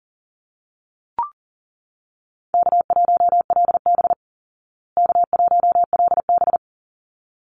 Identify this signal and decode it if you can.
Morse code, heard as a keyed tone, about 40 wpm, 700 Hz. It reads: K1LB K1LB
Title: Easy Call Signs Repeated 2x - 40wpm